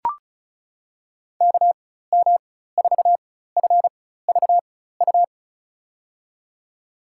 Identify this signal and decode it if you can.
Morse code, heard as a keyed tone, about 36 wpm, spaced as for 16 wpm, 700 Hz. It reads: KM4FVU
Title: USA Call Signs - 4x Character Spacing - 35wpm